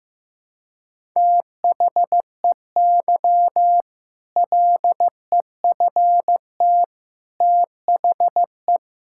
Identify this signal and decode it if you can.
Morse code, heard as a keyed tone, about 16 wpm, 700 Hz. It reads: THEY LEFT THE
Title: Sentences from Top 1000 Words - Review Part 6 - 15wpm